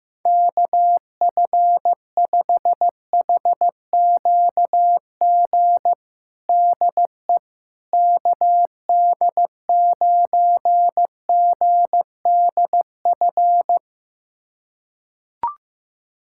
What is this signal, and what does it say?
Morse code, heard as a keyed tone, about 16 wpm, 700 Hz. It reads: KF5HQG DE KD9GDF
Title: USA Callsign DE Callsign 15wpm